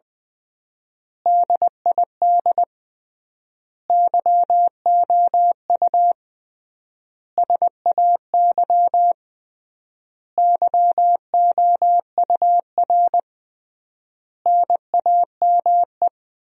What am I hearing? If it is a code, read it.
Morse code, heard as a keyed tone, about 20 wpm, 700 Hz. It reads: DID YOU SAY YOUR NAME
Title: 3x Word-Spacing - Sentences from Top 200 Words - 20wpm